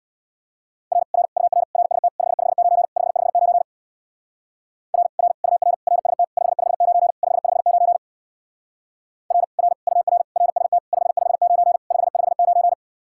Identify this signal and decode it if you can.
Morse code, heard as a keyed tone, about 60 wpm, 700 Hz. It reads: R R UR RST 559 559 R R UR RST 559 559 R R UR RST 559 559
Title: Speed-Racing Common QSO Phrases - 50wpm